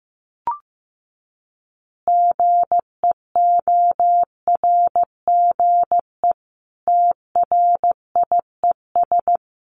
Morse code, heard as a keyed tone, about 15 wpm, 700 Hz. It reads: GEORGE TRIES
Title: Additional Easy Phrases 15wpm